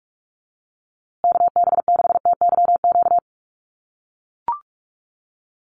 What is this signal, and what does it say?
Morse code, heard as a keyed tone, about 44 wpm, 700 Hz. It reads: KB6TYQ
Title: USA Call Signs Repeated 2x 45wpm